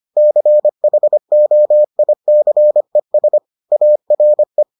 Morse code, heard as a keyed tone, about 25 wpm, 600 Hz. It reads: CHOICES ARE